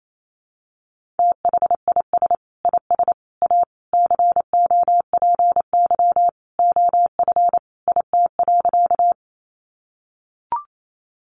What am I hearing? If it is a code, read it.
Morse code, heard as a keyed tone, about 28 wpm, 700 Hz. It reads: THIS IS A COPY OF IT.
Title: Sentences from Top 1400 Words 28wpm